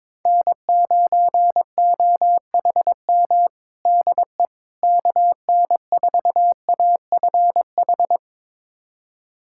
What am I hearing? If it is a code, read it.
Morse code, heard as a keyed tone, about 21 wpm, 700 Hz. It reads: N9OHM DE KN4AFH
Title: USA Callsign DE Callsign 22wpm